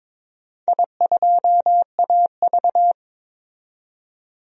Morse code, heard as a keyed tone, about 21 wpm, 690 Hz. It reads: I2AV
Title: DXCC-100 Call Signs - 4 Characters - 22wpm